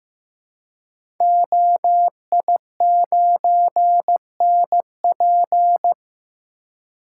Morse code, heard as a keyed tone, about 15 wpm, 700 Hz. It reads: OI9NP
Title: Code Groups - 5 Letter:Numbers - 15wpm